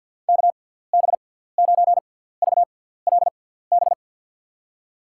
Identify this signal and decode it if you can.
Morse code, heard as a keyed tone, about 54 wpm, spaced as for 17 wpm, 700 Hz. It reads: KB9VLB
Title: USA Call Signs - 6x Character Spacing - 50wpm